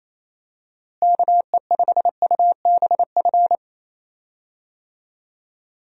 Morse code, heard as a keyed tone, about 27 wpm, 700 Hz. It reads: KE5UBF
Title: USA Call Signs - 6 Characters - Repeated 2x - 28wpm